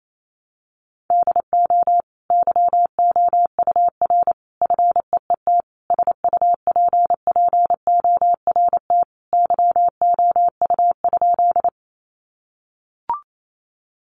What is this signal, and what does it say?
Morse code, heard as a keyed tone, about 28 wpm, 700 Hz. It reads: DO YOUR FEET SUPPORT YOU?
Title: Sentences from Top 800 Words 28wpm